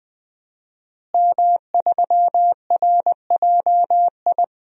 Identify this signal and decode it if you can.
Morse code, heard as a keyed tone, about 20 wpm, 700 Hz. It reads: M3RJI
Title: DXCC-100 Call Signs - 5 Characters - Repeated 2x - 20wpm